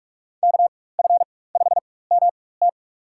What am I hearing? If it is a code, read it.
Morse code, heard as a keyed tone, about 46 wpm, spaced as for 21 wpm, 700 Hz. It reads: KF5MT
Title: USA Call Signs - 5 Characters - 4x Character Spacing - 45wpm